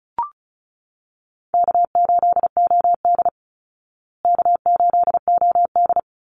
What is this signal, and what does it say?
Morse code, heard as a keyed tone, about 34 wpm, 700 Hz. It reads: K8OD K8OD
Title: Easy Call Signs Repeated 2x - 35wpm